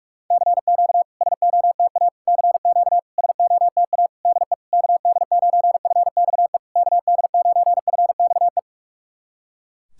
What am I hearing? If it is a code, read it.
Morse code, heard as a keyed tone, about 46 wpm, 700 Hz. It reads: CQ SOTA CQ SOTA DE KD9FXE KD9FXE
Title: SOTA QSO Practice 45wpm